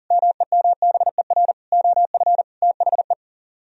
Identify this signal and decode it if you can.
Morse code, heard as a keyed tone, about 53 wpm, 700 Hz. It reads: MEMBER OF THE